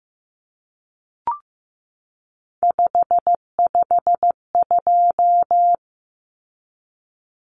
Morse code, heard as a keyed tone, about 15 wpm, 700 Hz. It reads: 552
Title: Numbers - 3-Digit 15wpm